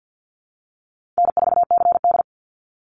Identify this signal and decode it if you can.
Morse code, heard as a keyed tone, about 50 wpm, 700 Hz. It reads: N4CD
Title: Easy Call Signs - 50wpm